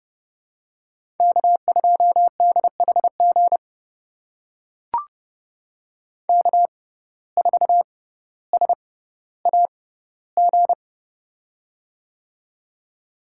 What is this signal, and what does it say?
Morse code, heard as a keyed tone, about 30 wpm, spaced as for 15 wpm, 700 Hz. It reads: K2DHG K4SAG
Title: USA Call Signs - 5 Characters - 6x Character Spacing - 30wpm